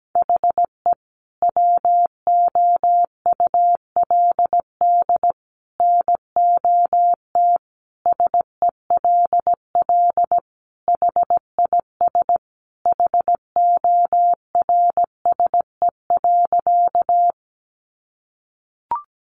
Morse code, heard as a keyed tone, about 17 wpm, 700 Hz. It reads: HE WOULD NOT SELL HIS HORSE.
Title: Sentences from Top 1100 Words 17wpm